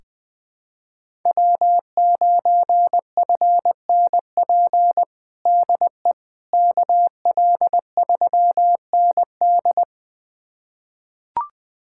Morse code, heard as a keyed tone, about 20 wpm, 700 Hz. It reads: W9FNP DE KL3ND
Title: USA Callsign DE Callsign - 5 Characters - 20wpm